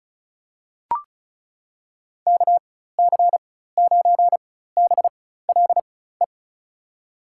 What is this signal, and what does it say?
Morse code, heard as a keyed tone, about 36 wpm, spaced as for 16 wpm, 700 Hz. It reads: KC9BLE
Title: USA Call Signs - 6 Characters - 4x Character Spacing - 35wpm